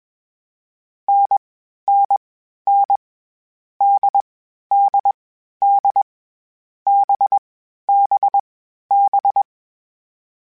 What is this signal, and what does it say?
Morse code, heard as a keyed tone, about 21 wpm, 800 Hz. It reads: N N N D D D B B B
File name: Short Runs / Short Runs-mp3s / NDB NDB6 @20